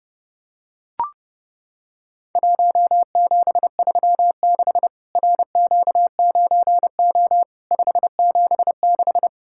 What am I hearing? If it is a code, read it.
Morse code, heard as a keyed tone, about 30 wpm, 700 Hz. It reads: 1736 RQ9O 576
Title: Worked All Europe DX Contest - QTC Exchange - 30wpm